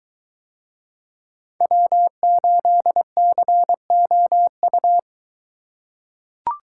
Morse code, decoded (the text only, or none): W8COU